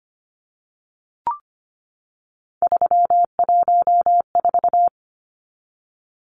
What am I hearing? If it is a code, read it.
Morse code, heard as a keyed tone, about 25 wpm, 690 Hz. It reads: 314
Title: Numbers - 3-Digit 25wpm